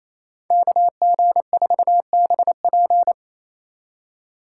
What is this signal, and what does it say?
Morse code, heard as a keyed tone, about 36 wpm, 700 Hz. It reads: KG4BP